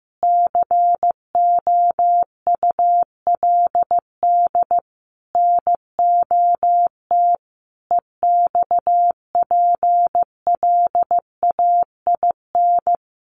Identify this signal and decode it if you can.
Morse code, heard as a keyed tone, about 15 wpm, 700 Hz. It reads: COULD NOT EXPLAIN
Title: Sentences from Top 1500 Words 15wpm